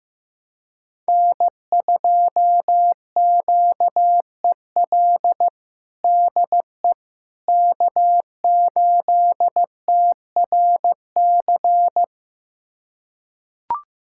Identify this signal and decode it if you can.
Morse code, heard as a keyed tone, about 16 wpm, 700 Hz. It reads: N2QEL DE K8TRC
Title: USA Callsign DE Callsign - 5 Characters - 15wpm